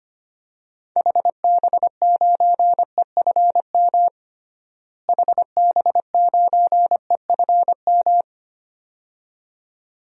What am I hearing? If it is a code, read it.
Morse code, heard as a keyed tone, about 25 wpm, 700 Hz. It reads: HB9EFM HB9EFM
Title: DXCC-100 Call Signs - 6 Characters - Repeated 2x - 25wpm